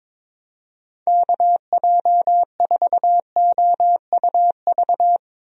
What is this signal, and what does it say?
Morse code, heard as a keyed tone, about 22 wpm, 700 Hz. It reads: KJ4OUV
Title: USA Call Signs - 6 Characters - Repeated 3x - 22wpm